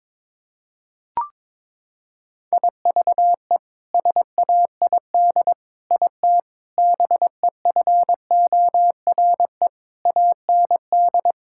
Morse code, heard as a keyed tone, about 22 wpm, 700 Hz. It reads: IVE SAID IT BEFORE AND
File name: practice / 22wpm / Sets of 8 Words - 22wpm